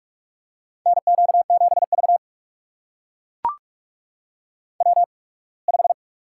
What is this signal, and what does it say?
Morse code, heard as a keyed tone, about 44 wpm, spaced as for 27 wpm, 690 Hz. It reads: NQ7V W5
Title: USA Call Signs - 4 Characters - 8x Character Spacing - 45wpm